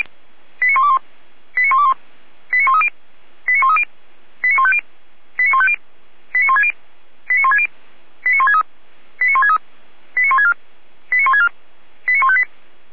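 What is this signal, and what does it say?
Five-tone selective call (ZVEI), heard as quick runs of five tones, tones of about 70 ms. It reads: 89121 89121 89130 89130 89150 89150 89160 89160 89163 89163 89164 89164 89158
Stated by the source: ZVEI five-tone call received by SDR, from server of Feuerwehr Orlamünde fire brigade.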